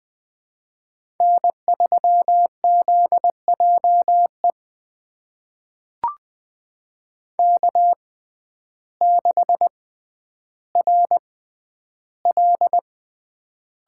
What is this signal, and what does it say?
Morse code, heard as a keyed tone, about 20 wpm, spaced as for 11 wpm, 700 Hz. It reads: N3ZJE K6RL
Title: USA Call Signs - 5 Characters - 6x Character Spacing - 20wpm